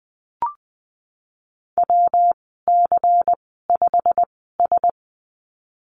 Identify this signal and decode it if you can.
Morse code, heard as a keyed tone, about 20 wpm, spaced as for 15 wpm, 700 Hz. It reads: WC5S
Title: 140 Lesson 23g - Letter f - Call Signs 20wpm